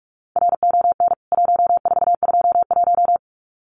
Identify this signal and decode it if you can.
Morse code, heard as a keyed tone, about 44 wpm, 700 Hz. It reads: RON 1421
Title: CWT Exchange 45wpm